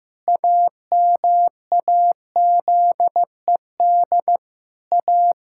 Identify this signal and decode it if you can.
Morse code, heard as a keyed tone, about 15 wpm, 700 Hz. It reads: AMAZED A
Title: Pangram Challenge v2 - 15wpm